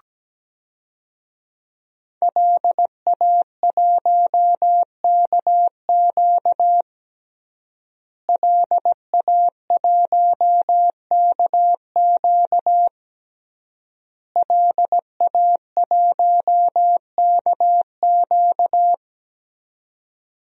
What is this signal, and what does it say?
Morse code, heard as a keyed tone, about 17 wpm, 700 Hz. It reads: LA1KQ LA1KQ LA1KQ
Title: DXCC-100 Call Signs - 5 Characters - Repeated 3x - 17wpm